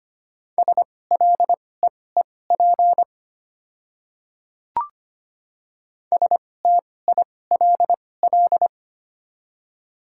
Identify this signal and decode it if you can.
Morse code, heard as a keyed tone, about 25 wpm, spaced as for 18 wpm, 700 Hz. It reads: SLEEP STILL
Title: 2x Character Spacing - Top 500 Words - 25wpm